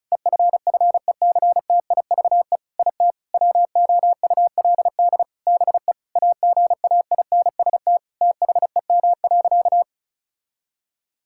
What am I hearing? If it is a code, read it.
Morse code, heard as a keyed tone, about 34 wpm, 700 Hz. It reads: EFFECTIVE IT WOULD BE AGAINST THEM.
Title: Sentences from Top 1000 Words - Review Part 6 - 35wpm